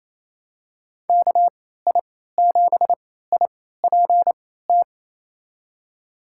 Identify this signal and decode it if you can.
Morse code, heard as a keyed tone, about 28 wpm, spaced as for 16 wpm, 700 Hz. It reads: KI7IPT
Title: USA Call Signs - 6 Characters - 3x Character Spacing - 28wpm